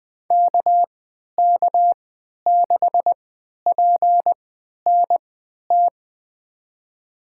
Morse code, heard as a keyed tone, about 20 wpm, spaced as for 11 wpm, 700 Hz. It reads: KK6PNT
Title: USA Call Signs - 3x Character Spacing - 20wpm